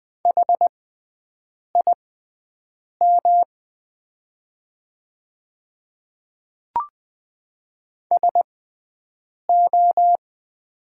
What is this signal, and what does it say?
Morse code, heard as a keyed tone, about 20 wpm, spaced as for 7 wpm, 700 Hz. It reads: HIM SO
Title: 6x Character Spacing - Top 200 Words - 20wpm